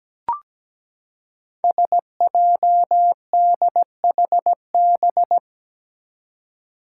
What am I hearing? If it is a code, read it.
Morse code, heard as a keyed tone, about 17 wpm, 700 Hz. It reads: SJDHB
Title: Five Challenging Letter-Number Combinations 17wpm